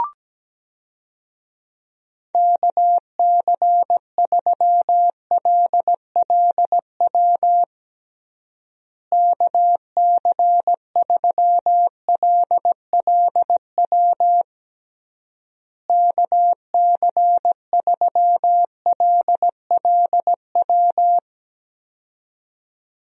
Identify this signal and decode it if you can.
Morse code, heard as a keyed tone, about 17 wpm, 700 Hz. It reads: KC3LLW KC3LLW KC3LLW
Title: USA Call Signs - 6 Characters - Repeated 3x - 17wpm